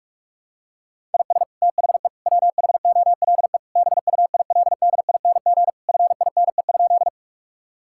Morse code, heard as a keyed tone, about 41 wpm, 700 Hz. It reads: IS THE WHOLE BUILDING FINE?
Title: Sentences from Top 1000 Words - Review Part 5 - 45wpm